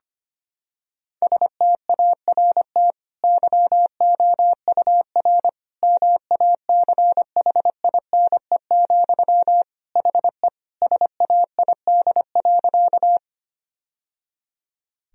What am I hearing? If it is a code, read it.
Morse code, heard as a keyed tone, about 25 wpm, 700 Hz. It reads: START YOUR MACHINE, HE SAID.
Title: Sentences from Top 1000 Words - Review Part 2 - 25wpm